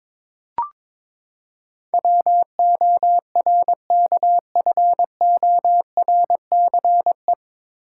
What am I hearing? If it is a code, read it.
Morse code, heard as a keyed tone, about 22 wpm, 700 Hz. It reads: WORKFORCE
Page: Morse Code Ninja practice 22wpm